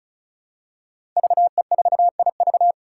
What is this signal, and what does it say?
Morse code, heard as a keyed tone, about 36 wpm, 700 Hz. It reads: VE4IV
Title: Five Challenging Letter-Number Combinations 35wpm